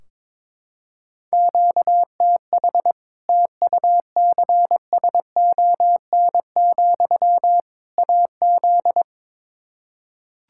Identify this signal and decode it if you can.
Morse code, heard as a keyed tone, about 22 wpm, 700 Hz. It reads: QTH TUCSON, AZ
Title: Practice QSO - Exchange 1 - Beginning - 22wpm